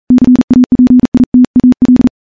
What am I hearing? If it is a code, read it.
Morse code, heard as a keyed tone, about 37 wpm, 250 Hz, with radio noise. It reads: CAPITAL
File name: morse-training-40WPM